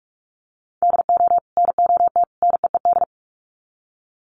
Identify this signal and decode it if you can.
Morse code, heard as a keyed tone, about 44 wpm, 700 Hz. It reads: DO NOT NEED
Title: Sets of 3 Words 45wpm